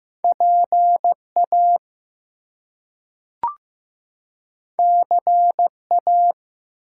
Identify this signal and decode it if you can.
Morse code, heard as a keyed tone, about 15 wpm, 700 Hz. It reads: PA CA
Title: US States and Canadian Provinces 15wpm